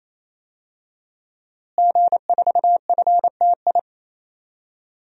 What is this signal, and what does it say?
Morse code, heard as a keyed tone, about 28 wpm, 700 Hz. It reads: G4FTI
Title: DXCC-100 Call Signs - 5 Characters - Repeated 3x - 28wpm